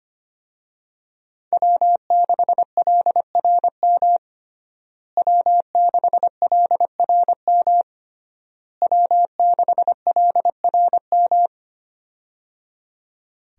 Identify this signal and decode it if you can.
Morse code, heard as a keyed tone, about 25 wpm, 700 Hz. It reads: W6LRM W6LRM W6LRM
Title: USA Call Signs - 5 Characters - Repeated 3x - 25wpm